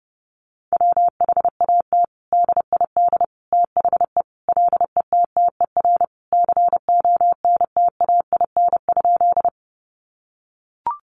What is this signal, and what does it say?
Morse code, heard as a keyed tone, about 30 wpm, 700 Hz. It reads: WHAT DID THE LETTER CONTAIN?
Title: Sentences from Top 500 Words 30wpm